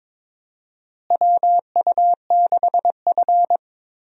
Morse code, heard as a keyed tone, about 22 wpm, 700 Hz. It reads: WU6F
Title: USA Call Signs - 4 Characters - Repeated 2x - 22wpm